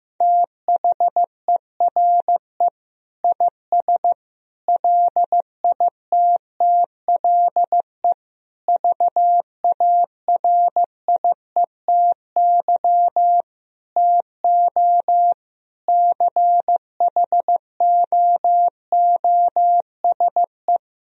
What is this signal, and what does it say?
Morse code, heard as a keyed tone, about 15 wpm, 700 Hz. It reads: THERE IS LITTLE VARIETY TO CHOOSE
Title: Sentences from Top 1000 Words - Review Part 4 - 15wpm